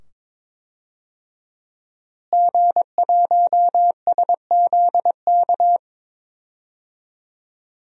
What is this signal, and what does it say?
Morse code, heard as a keyed tone, about 22 wpm, 700 Hz. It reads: G1SZK
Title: DXCC-100 Call Signs - 5 Characters - Repeated 2x - 22wpm